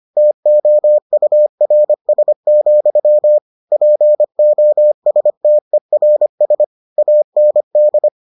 Morse code, heard as a keyed tone, about 25 wpm, 600 Hz. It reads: TOURS, POSTERS AND